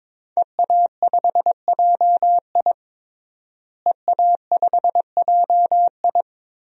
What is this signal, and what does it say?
Morse code, heard as a keyed tone, about 22 wpm, 700 Hz. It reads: EA5JI EA5JI
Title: DXCC-100 Call Signs - 5 Characters - Repeated 3x - 22wpm